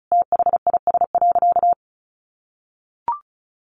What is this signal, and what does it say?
Morse code, heard as a keyed tone, about 34 wpm, 700 Hz. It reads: THIS.
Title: Sentences from Top 400 Words 35wpm